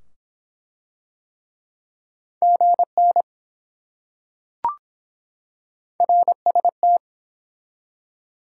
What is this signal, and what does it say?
Morse code, heard as a keyed tone, about 23 wpm, 700 Hz. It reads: GN RST